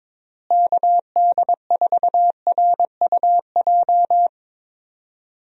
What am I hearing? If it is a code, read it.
Morse code, heard as a keyed tone, about 21 wpm, 700 Hz. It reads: KD4RUJ
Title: USA Call Signs - 6 Characters - Repeated 2x - 22wpm